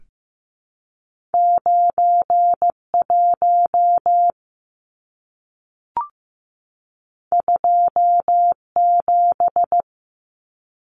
Morse code, heard as a keyed tone, about 15 wpm, 700 Hz. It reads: 9 1 2 7
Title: Numbers - 2-Digit 15wpm